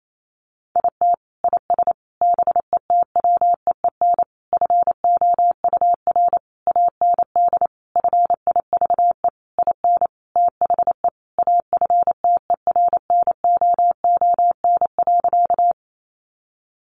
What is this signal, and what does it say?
Morse code, heard as a keyed tone, about 28 wpm, 700 Hz. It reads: IT IS BETWEEN FOUR AND FIVE IN THE AFTERNOON.
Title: Sentences from Top 1500 Words 28wpm